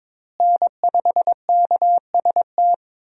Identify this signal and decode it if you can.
Morse code, heard as a keyed tone, about 22 wpm, 700 Hz. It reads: N5KST
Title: USA Call Signs - 5 Characters - Repeated 2x - 22wpm